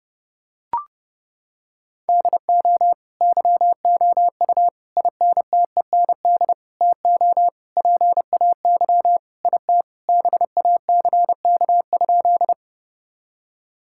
Morse code, heard as a keyed tone, about 32 wpm, 700 Hz. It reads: DO YOU INTEND TO PAY IT BACK?
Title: Sentences from Top 1600 Words 30wpm